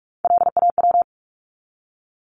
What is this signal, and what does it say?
Morse code, heard as a keyed tone, about 44 wpm, 700 Hz. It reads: LAW
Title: ES:S Suffix - Top 100 Words with Base Words 45wpm